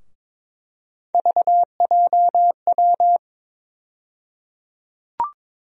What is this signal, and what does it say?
Morse code, heard as a keyed tone, about 22 wpm, 700 Hz. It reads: VJW
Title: USA Call Sign Postfixes - 3 Letters 22wpm